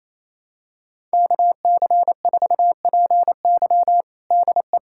Morse code, heard as a keyed tone, about 28 wpm, 700 Hz. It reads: KC4PY DE
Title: USA Callsign DE Callsign - 5 Characters - 28wpm